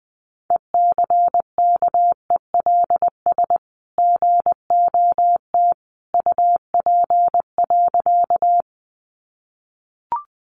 Morse code, heard as a keyed tone, about 20 wpm, 700 Hz. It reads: ECKELS GOT UP.
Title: A Sound Of Thunder 20wpm